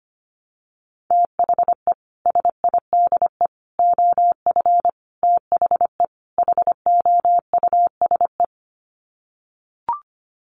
Morse code, heard as a keyed tone, about 25 wpm, 700 Hz. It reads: THE SIDE OF THE HOUSE
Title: Sets of 5 Words Encore2 25wpm